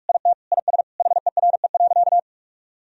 Morse code, heard as a keyed tone, about 46 wpm, 700 Hz. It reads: IT IS HERE.
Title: Sentences from Top 1000 Words - Review Part 2 - 45wpm